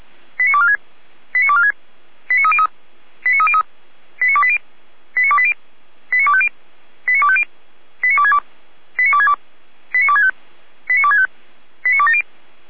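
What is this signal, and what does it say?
Five-tone selective call (ZVEI), heard as quick runs of five tones, tones of about 70 ms. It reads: 89246 89246 89303 89303 89280 89280 89250 89250 89262 89262 89265 89265 89270
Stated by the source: ZVEI five-tone selective call received on SDR.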